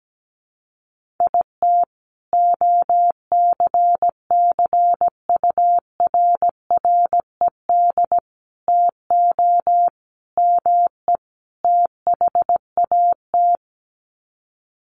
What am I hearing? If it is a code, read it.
Morse code, heard as a keyed tone, about 17 wpm, 700 Hz. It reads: IT OCCURRED TO ME THAT
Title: Sets of 5 Words Encore 17wpm